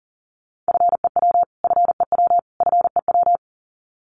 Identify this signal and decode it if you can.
Morse code, heard as a keyed tone, about 40 wpm, 700 Hz. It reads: FEW FEW FEW
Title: Top 500 Words Repeated 40wpm